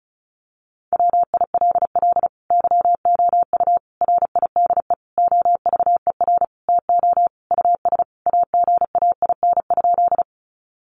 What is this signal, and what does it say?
Morse code, heard as a keyed tone, about 34 wpm, 700 Hz. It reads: WILL YOU RIDE OVER TO US AGAIN?
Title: Sentences from Top 1200 Words 35wpm